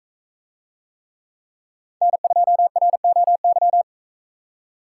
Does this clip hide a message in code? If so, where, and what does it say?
Yes: Morse code from 2.0 to 3.8 s, heard as a keyed tone, about 44 wpm, 700 Hz. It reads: N2ROY